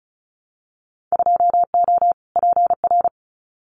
Morse code, heard as a keyed tone, about 34 wpm, 700 Hz. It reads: 2O PR
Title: Winter Field Day Exchanges Encore 35wpm